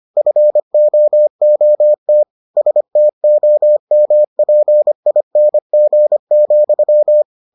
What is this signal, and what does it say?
Morse code, heard as a keyed tone, about 25 wpm, 600 Hz. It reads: FOOT STOMPING,